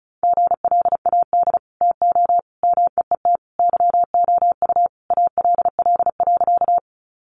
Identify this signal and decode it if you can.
Morse code, heard as a keyed tone, about 34 wpm, 700 Hz. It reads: GLAD TO MEET YOU ALL.
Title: Sentences from Top 1100 Words 35wpm